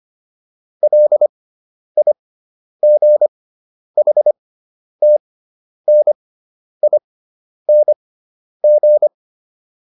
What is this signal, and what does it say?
Morse code, heard as a keyed tone, about 25 wpm, spaced as for 10 wpm, 600 Hz. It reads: LIGHTNING